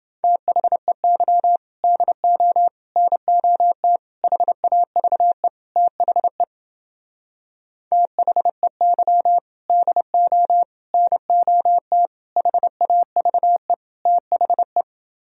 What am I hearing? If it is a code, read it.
Morse code, heard as a keyed tone, about 30 wpm, 700 Hz. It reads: THEY DO NOT HAVE THE THEY DO NOT HAVE THE